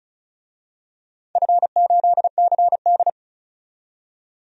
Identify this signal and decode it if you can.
Morse code, heard as a keyed tone, about 36 wpm, 700 Hz. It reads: F8CD
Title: DXCC-100 Call Signs - 4 Characters - 35wpm